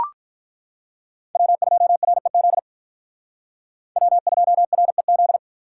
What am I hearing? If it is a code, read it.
Morse code, heard as a keyed tone, about 43 wpm, 700 Hz. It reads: W2REZ W2REZ
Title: Speed Racing - USA 1x3 Call Signs 35wpm